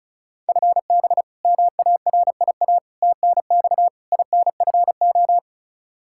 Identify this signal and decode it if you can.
Morse code, heard as a keyed tone, about 36 wpm, 700 Hz. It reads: FB MARIA TNX INFO
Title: Practice QSO - Exchange 3 - More Info - 35wpm